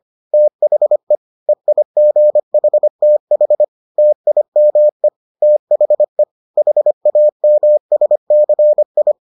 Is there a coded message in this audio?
Yes: Morse code from 0.3 to 9.1 s, heard as a keyed tone, about 25 wpm, 600 Hz. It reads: THE EIGHTH TIME THE HAMSCI